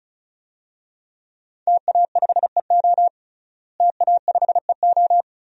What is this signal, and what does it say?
Morse code, heard as a keyed tone, about 36 wpm, 700 Hz. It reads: TA5EO TA5EO
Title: DXCC-100 Call Signs - 5 Characters - Repeated 2x - 35wpm